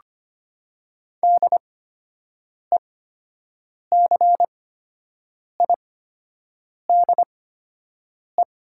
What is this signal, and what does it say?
Morse code, heard as a keyed tone, about 25 wpm, spaced as for 7 wpm, 700 Hz. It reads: DECIDE
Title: 8x Character Spacing - Top 500 Words - 25wpm